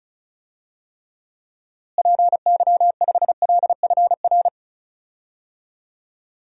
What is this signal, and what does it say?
Morse code, heard as a keyed tone, about 36 wpm, 700 Hz. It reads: PY5LFR
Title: DXCC-100 Call Signs - 6 Characters - Repeated 2x - 35wpm